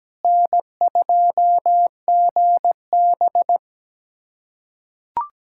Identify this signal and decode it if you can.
Morse code, heard as a keyed tone, about 17 wpm, 700 Hz. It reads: N2GB
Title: USA Callsign DE Callsign - 4 Characters - 17wpm